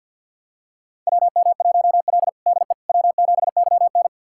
Morse code, heard as a keyed tone, about 48 wpm, 700 Hz. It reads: WM1L DE W7YN
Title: USA Callsign DE Callsign - 4 Characters - 50wpm